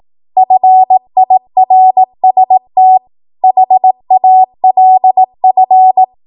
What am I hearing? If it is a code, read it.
Morse code, heard as a keyed tone, about 18 wpm, 750 Hz. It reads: FIRST HALF
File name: Morse / 18WPM